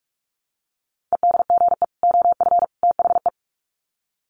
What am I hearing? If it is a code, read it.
Morse code, heard as a keyed tone, about 44 wpm, 700 Hz. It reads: EDGE OF THE
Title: Sets of 3 Words 45wpm